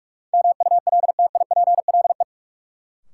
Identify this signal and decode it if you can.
Morse code, heard as a keyed tone, about 47 wpm, 700 Hz. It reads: MULTIPLE
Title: Sets of 5 Words - New Word Drill - 45wpm